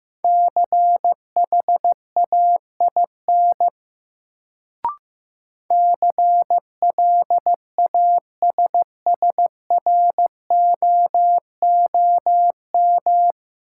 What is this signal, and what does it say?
Morse code, heard as a keyed tone, about 15 wpm, 700 Hz. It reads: CHAIN CLASSROOM
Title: Sentences from Top 1500 Words - New Word Drill Alt - 15wpm